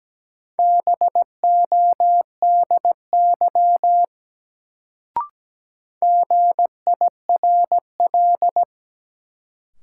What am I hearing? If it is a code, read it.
Morse code, heard as a keyed tone, about 17 wpm, 700 Hz. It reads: BODY GIRL
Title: Sentences from Top 400 Words - New Word Drill - 17wpm